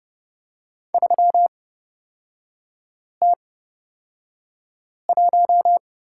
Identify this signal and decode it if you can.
Morse code, heard as a keyed tone, about 30 wpm, spaced as for 5 wpm, 700 Hz. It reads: 3T1
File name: Lesson 36e - BT - Three Characters 30wpm